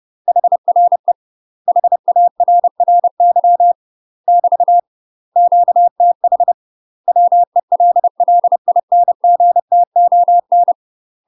Morse code, heard as a keyed tone, about 30 wpm, 700 Hz. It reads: HRE HARRY <BT> QTH WELLINGTON
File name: UK MASTER QSOs TRAINING / 30WPM / UK MASTER QSOs 0020